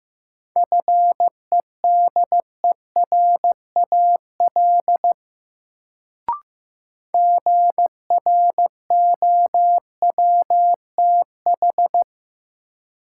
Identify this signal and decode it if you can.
Morse code, heard as a keyed tone, about 15 wpm, 700 Hz. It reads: FEDERAL GROWTH
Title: Sentences from Top 1000 Words - Review Part 5 - New Word Drill - 15wpm